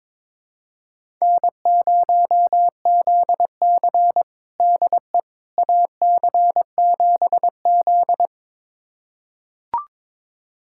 Morse code, heard as a keyed tone, about 21 wpm, 700 Hz. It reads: N0ZC DE AC7Z
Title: USA Callsign DE Callsign - 4 Characters - 22wpm